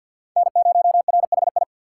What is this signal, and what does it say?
Morse code, heard as a keyed tone, about 48 wpm, 700 Hz. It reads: N0RHI
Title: USA Call Signs - 5 Characters - 6x Character Spacing - 50wpm